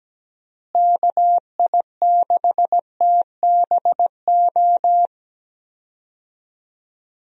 Morse code, heard as a keyed tone, about 17 wpm, 700 Hz. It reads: KI6TBO
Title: USA Call Signs - 6 Characters - Repeated 3x - 17wpm